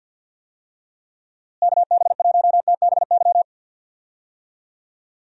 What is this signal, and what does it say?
Morse code, heard as a keyed tone, about 51 wpm, 700 Hz. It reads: KB1TBY
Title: USA Call Signs - 6 Characters - Repeated 3x - 50wpm